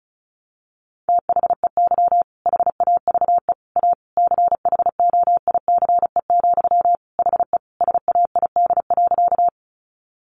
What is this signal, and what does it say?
Morse code, heard as a keyed tone, about 34 wpm, 700 Hz. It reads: THEY HAVE A CHOICE, HE SAID.
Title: Sentences from Top 900 Words 35wpm